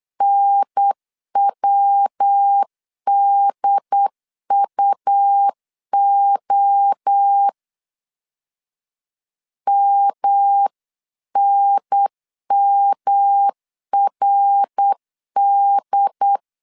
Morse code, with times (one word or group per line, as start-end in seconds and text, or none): 0.2-7.5 NWDUO
9.7-16.4 MNMRD